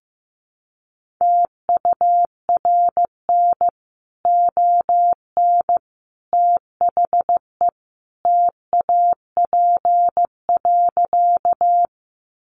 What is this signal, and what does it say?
Morse code, heard as a keyed tone, about 15 wpm, 700 Hz. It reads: TURN ON THE TAP.
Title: Sentences from Top 1400 Words 15wpm